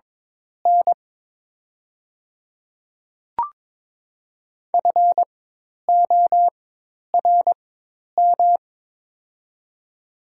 Morse code, heard as a keyed tone, about 23 wpm, spaced as for 10 wpm, 700 Hz. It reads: N FORM
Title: 4x Character Spacing - Top 300 Words - 22wpm